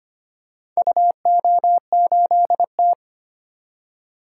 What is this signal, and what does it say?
Morse code, heard as a keyed tone, about 25 wpm, 700 Hz. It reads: UO8T
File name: practice / 25wpm / Four Letter-Number Combinations - 25wpm